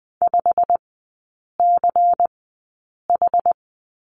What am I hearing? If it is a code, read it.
Morse code, heard as a keyed tone, about 20 wpm, spaced as for 8 wpm, 700 Hz. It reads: 5CH
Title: 093 Lesson 16e - Letter c - Three Characters 20wpm